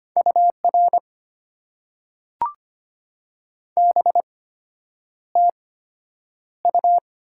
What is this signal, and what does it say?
Morse code, heard as a keyed tone, about 25 wpm, spaced as for 9 wpm, 700 Hz. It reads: UR BTU
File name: Top 90 QSO Elements - 8x Character Spacing - 25wpm